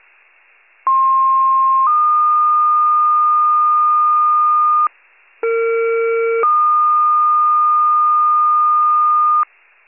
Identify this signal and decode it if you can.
Two-tone page, heard as a one-second tone, then a three-second tone, about 1060 then 1220 Hz, about 460 then 1220 Hz. It is heard from 0.9 to 9.4 s.